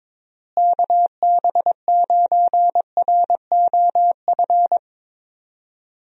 Morse code, heard as a keyed tone, about 22 wpm, 700 Hz. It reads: KB9ROF
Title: USA Call Signs - 6 Characters - Repeated 3x - 22wpm